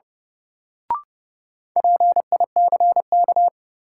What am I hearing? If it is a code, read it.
Morse code, heard as a keyed tone, about 30 wpm, 700 Hz. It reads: PICK
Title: Sentences from Top 1000 Words - New Word Drill - 30wpm